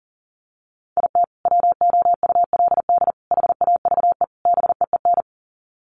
Morse code, heard as a keyed tone, about 37 wpm, 700 Hz. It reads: IT WOULD HAVE BEEN
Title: Sets of 4 Words 40wpm